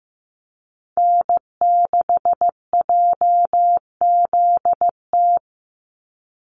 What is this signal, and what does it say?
Morse code, heard as a keyed tone, about 15 wpm, 700 Hz. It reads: N6JZT
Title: NA SOTA Activator Callsigns 15wpm